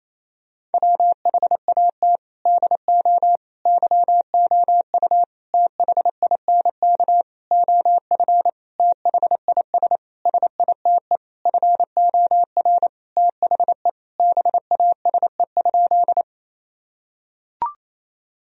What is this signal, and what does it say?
Morse code, heard as a keyed tone, about 28 wpm, 700 Hz. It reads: WHAT DO YOU THINK OF THIS SITE FOR THE BASE?
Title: Sentences from Top 1000 Words - Review Part 1 - 28wpm